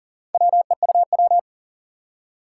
Morse code, heard as a keyed tone, about 40 wpm, 700 Hz. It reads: WEUW
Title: Four Challenging Letter-Number Combinations 40wpm